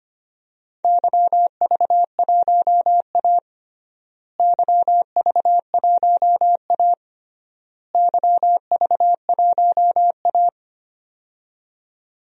Morse code, heard as a keyed tone, about 25 wpm, 700 Hz. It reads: YV1A YV1A YV1A
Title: DXCC-100 Call Signs - 4 Characters - Repeated 3x - 25wpm